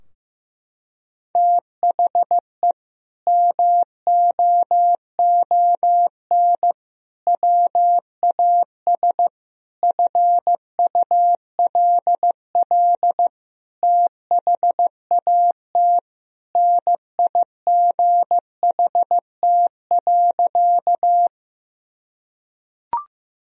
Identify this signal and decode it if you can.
Morse code, heard as a keyed tone, about 16 wpm, 700 Hz. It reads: THE MOON WAS FULL THAT NIGHT.
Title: Sentences from Top 1000 Words - Review Part 3 - 15wpm